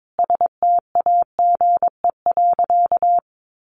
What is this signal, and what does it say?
Morse code, heard as a keyed tone, about 22 wpm, 700 Hz. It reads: STAGE.
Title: Sentences from Top 1200 Words 22wpm